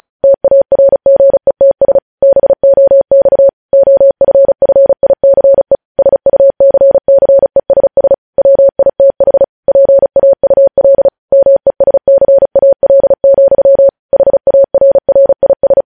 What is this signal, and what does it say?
Morse code, heard as a keyed tone, about 36 wpm, 550 Hz. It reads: TARGETS BOX OFFICE SUCCESS WITH PAUL MESCAL, HARRIS
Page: News Headlines in Morse Code at 35 WPM